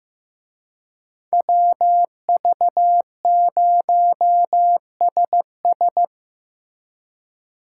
Morse code, heard as a keyed tone, about 17 wpm, 700 Hz. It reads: WV0SS
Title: Speed Racing - USA 2x2 Call Signs 15wpm